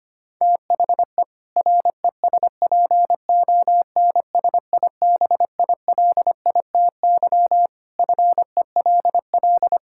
Morse code, heard as a keyed tone, about 25 wpm, 700 Hz. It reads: THE RESPONSIBILITY FELL
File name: practice / 25wpm / Sentences with 2-Word Phrases - 25wpm